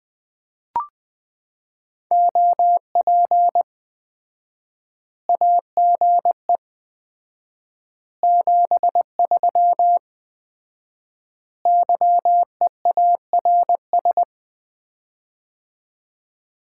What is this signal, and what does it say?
Morse code, heard as a keyed tone, about 20 wpm, 700 Hz. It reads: OP AGE 73 YEARS
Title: Common QSO Phrases - 4x Word-Spacing - 20wpm